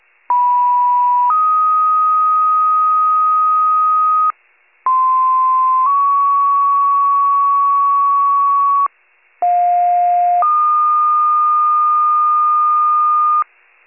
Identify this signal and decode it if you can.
Two-tone page, heard as a one-second tone, then a three-second tone, about 980 then 1250 Hz, about 1040 then 1120 Hz, about 710 then 1220 Hz. It is heard from 0.3 to 13.4 s.
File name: pager / 46 RESCUE / wcalert